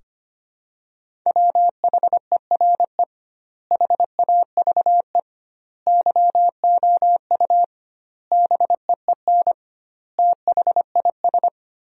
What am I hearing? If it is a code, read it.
Morse code, heard as a keyed tone, about 25 wpm, 700 Hz. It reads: WHERE HAVE YOU BEEN THIS
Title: 2x Word-Spacing - Sentences from Top 300 Words - 25wpm